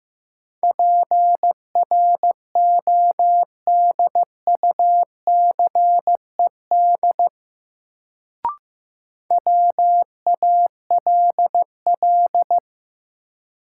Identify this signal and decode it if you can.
Morse code, heard as a keyed tone, about 15 wpm, 700 Hz. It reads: PRODUCED WALL
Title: Sentences from Top 1000 Words - Review Part 6 - New Word Drill - 15wpm